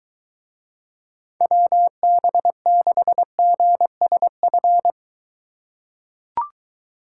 Morse code, heard as a keyed tone, about 21 wpm, 700 Hz. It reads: WB6GSF